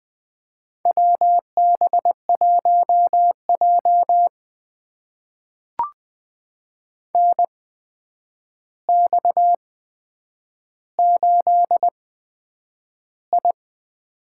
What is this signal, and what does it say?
Morse code, heard as a keyed tone, about 20 wpm, spaced as for 9 wpm, 700 Hz. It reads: WB1J NX8I
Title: USA Call Signs - 4 Characters - 8x Character Spacing - 20wpm